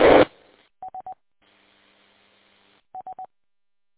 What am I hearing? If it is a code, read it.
There is no code.